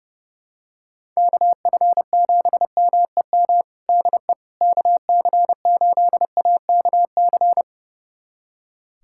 Morse code, heard as a keyed tone, about 32 wpm, 700 Hz. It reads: KF7MEM DE KC8AKC
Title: USA Callsign DE Callsign - 6 Characters - 30wpm